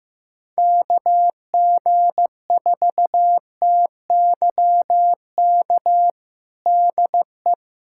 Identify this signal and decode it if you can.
Morse code, heard as a keyed tone, about 15 wpm, 700 Hz. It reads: KG4TYK DE
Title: USA Callsign DE Callsign - 6 Characters - 15wpm